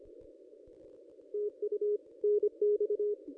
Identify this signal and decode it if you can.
Morse code, heard as a keyed tone, about 26 wpm, 420 Hz. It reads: TU NX
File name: contests / ss-cw-k5tr-op-n5rz -1400